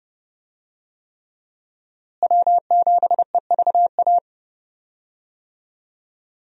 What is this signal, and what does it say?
Morse code, heard as a keyed tone, about 30 wpm, 700 Hz. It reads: W7EVA
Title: USA Call Signs - 5 Characters - Repeated 3x - 30wpm